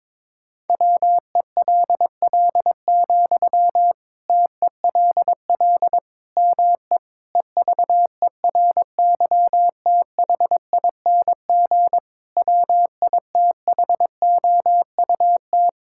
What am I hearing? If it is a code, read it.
Morse code, heard as a keyed tone, about 22 wpm, 700 Hz. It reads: WELL, TELL ME EVERYTHING WITHOUT
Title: Sentences from Top 1000 Words - Review Part 6 - 22wpm